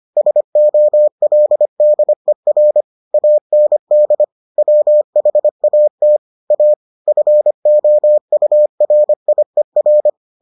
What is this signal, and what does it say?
Morse code, heard as a keyed tone, about 25 wpm, 600 Hz. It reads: SOLDER AND WHAT A FOURIER